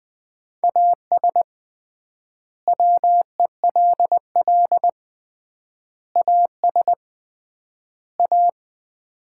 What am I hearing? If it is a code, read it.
Morse code, heard as a keyed tone, about 20 wpm, 700 Hz. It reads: AS WELL AS A
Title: 3x Word-Spacing - Sets of 4 Words - 20wpm